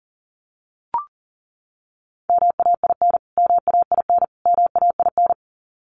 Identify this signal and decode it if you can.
Morse code, heard as a keyed tone, about 40 wpm, 700 Hz. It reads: MAIN MAIN MAIN
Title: Top 500 Words Repeated 40wpm